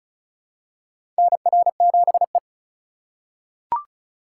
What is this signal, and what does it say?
Morse code, heard as a keyed tone, about 36 wpm, 700 Hz. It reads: NR7E
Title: USA Call Signs - 4 Characters - 3x Character Spacing - 35wpm